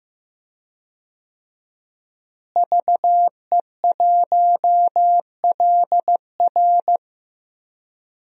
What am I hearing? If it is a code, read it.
Morse code, heard as a keyed tone, about 15 wpm, 700 Hz. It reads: VE1LR
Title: DXCC-100 Call Signs - 5 Characters - Repeated 2x - 15wpm